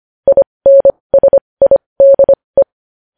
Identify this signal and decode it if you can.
Morse code, heard as a keyed tone, about 25 wpm, spaced as for 20 wpm, 550 Hz. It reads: INSIDE
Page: news-headlines-in-morse-code-at-20-wpm